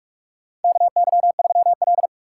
Morse code, heard as a keyed tone, about 47 wpm, 700 Hz. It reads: KY3L
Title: USA Call Signs - 4 Characters - 3x Character Spacing - 45wpm